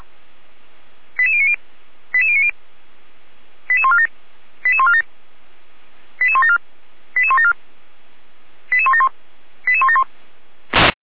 Five-tone selective call (ZVEI), heard as quick runs of five tones, tones of about 70 ms. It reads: 88090 88090 88147 88147 88174 88174 88171 88171